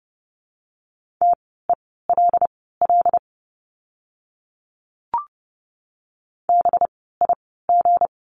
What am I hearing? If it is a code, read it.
Morse code, heard as a keyed tone, about 30 wpm, spaced as for 17 wpm, 700 Hz. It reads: TELL BIG
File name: course / 30:10wpm / 237 Lesson 38f - Letter q - Words 30wpm